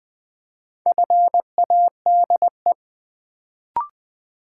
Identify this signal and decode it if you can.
Morse code, heard as a keyed tone, about 20 wpm, 700 Hz. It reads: FADE
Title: Build A Word - 2 Letters At A Time - 4-Letter Words - 20wpm